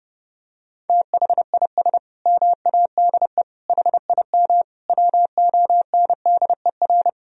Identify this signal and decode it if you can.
Morse code, heard as a keyed tone, about 28 wpm, 700 Hz. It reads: THIS MADE HIM WONDER
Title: Sentences from Top 500 Words and Abbreviations 30wpm